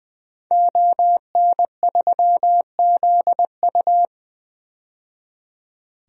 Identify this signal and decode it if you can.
Morse code, heard as a keyed tone, about 20 wpm, 700 Hz. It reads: ON3ZU
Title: DXCC-100 Call Signs - 5 Characters - 20wpm